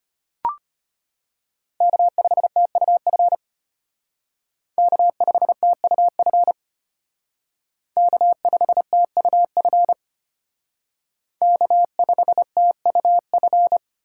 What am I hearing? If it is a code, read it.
Morse code, heard as a keyed tone, about 31 wpm, 700 Hz. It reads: K5TUF K5TUF K5TUF K5TUF
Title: Speed Racing - USA 1x3 Call Signs 25wpm